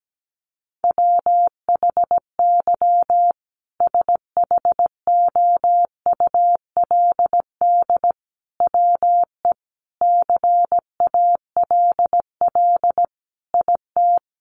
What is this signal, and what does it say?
Morse code, heard as a keyed tone, about 17 wpm, 700 Hz. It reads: WHY SHOULD WE CALL IT
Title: Sentences from Top 1100 Words 17wpm